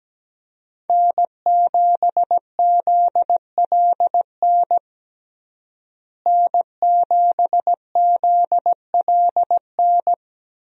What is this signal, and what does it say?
Morse code, heard as a keyed tone, about 17 wpm, 700 Hz. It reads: N7ZLN N7ZLN
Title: USA Call Signs - 5 Characters - Repeated 3x - 17wpm